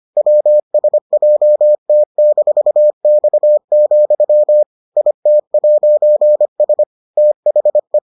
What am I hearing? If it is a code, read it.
Morse code, heard as a keyed tone, about 25 wpm, 600 Hz. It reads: WSJT-X, IT'S THE